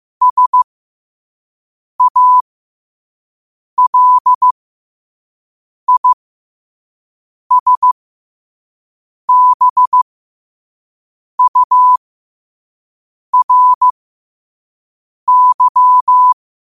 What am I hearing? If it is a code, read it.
Morse code, heard as a keyed tone, about 15 wpm, spaced as for 5 wpm, 1000 Hz. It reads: SALISBURY
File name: qotdfiles / 5wpm